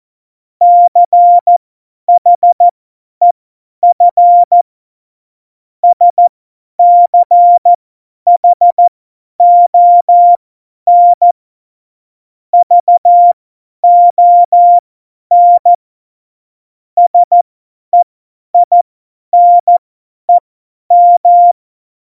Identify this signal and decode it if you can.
Morse code, heard as a keyed tone, about 14 wpm, spaced as for 10 wpm, 700 Hz. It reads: CHEF SCHON VON SEINEM
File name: Morsekurs / Witze / Witz 26 Mai 2020, 50 F70